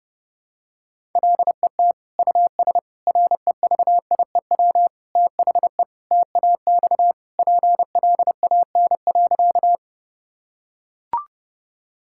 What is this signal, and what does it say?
Morse code, heard as a keyed tone, about 30 wpm, 700 Hz. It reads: LET US REVIEW THE TAX PLAN.
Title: Sentences from Top 1000 Words - Review Part 6 - 30wpm